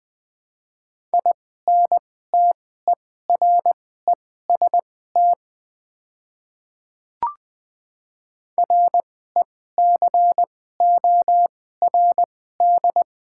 Morse code, heard as a keyed tone, about 20 wpm, spaced as for 14 wpm, 700 Hz. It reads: INTEREST RECORD
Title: 2x Character Spacing - Top 500 Words - 20wpm